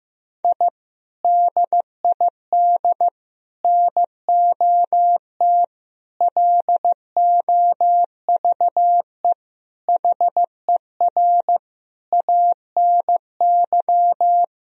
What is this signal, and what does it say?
Morse code, heard as a keyed tone, about 15 wpm, 700 Hz. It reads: I DID NOT LOVE HER ANY
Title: Sentences from Top 1000 Words - Review Part 2 - 15wpm